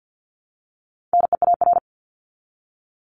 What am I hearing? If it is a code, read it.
Morse code, heard as a keyed tone, about 50 wpm, 700 Hz. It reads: NEAR
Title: LY Suffix - Top 100 Words with Base Words 50wpm